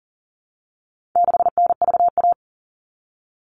Morse code, heard as a keyed tone, about 40 wpm, 700 Hz. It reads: 6NVA
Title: All Cut Numbers - 4-Digit 40wpm